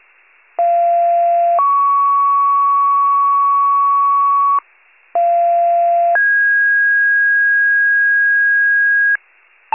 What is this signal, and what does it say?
Two-tone page, heard as a one-second tone, then a three-second tone, about 690 then 1120 Hz, about 690 then 1650 Hz, with noise. It is heard from 0.6 to 9.2 s.
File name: pager / 41 RESCUE DUTY / wcalert